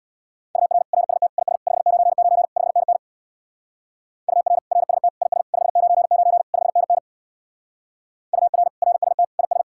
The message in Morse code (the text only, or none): UR RST IS 599 5NN UR RST IS 599 5NN UR RST IS